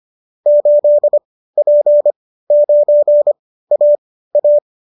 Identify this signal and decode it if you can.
Morse code, heard as a keyed tone, about 25 wpm, spaced as for 15 wpm, 590 Hz. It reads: 8P9AA